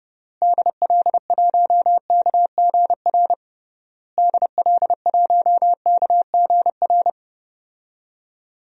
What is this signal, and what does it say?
Morse code, heard as a keyed tone, about 30 wpm, 700 Hz. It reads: DL1KGR DL1KGR
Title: DXCC-100 Call Signs - 6 Characters - Repeated 3x - 30wpm